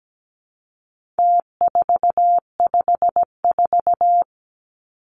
Morse code, heard as a keyed tone, about 17 wpm, 700 Hz. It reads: T454
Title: Numbers - Contest Style - 4-Digit 17wpm